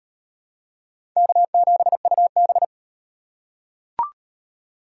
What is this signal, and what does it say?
Morse code, heard as a keyed tone, about 38 wpm, 700 Hz. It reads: K7UB